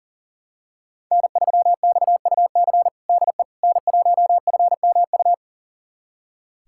Morse code, heard as a keyed tone, about 40 wpm, 700 Hz. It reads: N3XUC DE N1FMU
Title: USA Callsign DE Callsign - 5 Characters - 40wpm